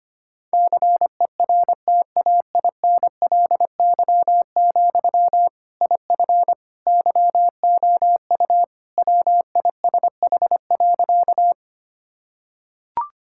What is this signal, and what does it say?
Morse code, heard as a keyed tone, about 25 wpm, 700 Hz. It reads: CERTAINLY, IF YOU WISH.
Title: Sentences from Top 1600 Words 25wpm